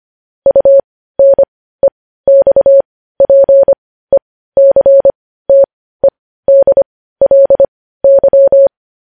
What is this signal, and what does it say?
Morse code, heard as a keyed tone, about 25 wpm, spaced as for 15 wpm, 550 Hz. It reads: UNEXPECTEDLY
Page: news headlines in morse code at 15 wpm